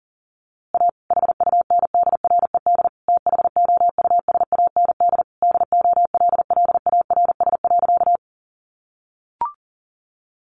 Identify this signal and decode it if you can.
Morse code, heard as a keyed tone, about 40 wpm, 700 Hz. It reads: A HUNDRED THOUSAND DOLLARS.
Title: A Sound Of Thunder 40wpm